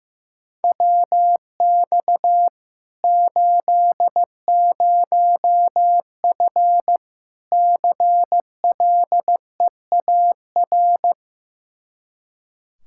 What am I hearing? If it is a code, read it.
Morse code, heard as a keyed tone, about 16 wpm, 700 Hz. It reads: WX 80F CLEAR
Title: Practice QSO - Exchange 2 - Info - 15wpm